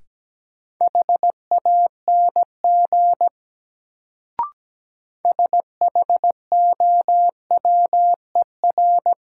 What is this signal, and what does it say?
Morse code, heard as a keyed tone, about 17 wpm, 700 Hz. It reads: HANG SHOWER
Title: Sentences from Top 1600 Words - New Word Drill - 17wpm